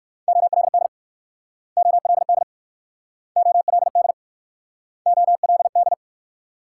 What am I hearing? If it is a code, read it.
Morse code, heard as a keyed tone, about 56 wpm, 700 Hz. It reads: OLD OLD OLD OLD
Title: Speed-Racing Multi-letter Phonemes - 45wpm